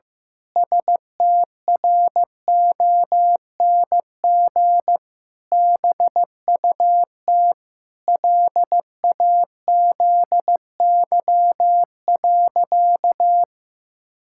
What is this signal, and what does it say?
Morse code, heard as a keyed tone, about 15 wpm, 700 Hz. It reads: STRONG BUT LAZY.
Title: Pangram Challenge v2 - 15wpm